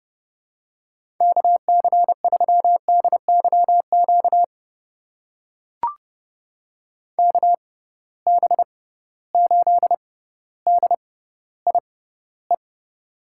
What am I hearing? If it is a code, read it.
Morse code, heard as a keyed tone, about 30 wpm, spaced as for 15 wpm, 700 Hz. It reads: KC3DYQ KB8DIE
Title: USA Call Signs - 6x Character Spacing - 30wpm